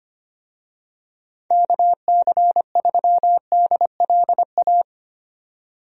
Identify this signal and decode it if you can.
Morse code, heard as a keyed tone, about 25 wpm, 700 Hz. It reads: KC3DLA
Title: USA Call Signs - 6 Characters - 25wpm